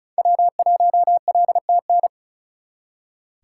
Morse code, heard as a keyed tone, about 36 wpm, 700 Hz. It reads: W1LTN
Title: Speed Racing - USA Call Signs 35wpm